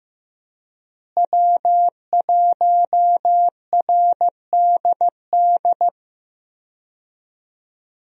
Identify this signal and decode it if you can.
Morse code, heard as a keyed tone, about 15 wpm, 700 Hz. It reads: W1RDD
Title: USA Call Signs - 5 Characters - Repeated 3x - 15wpm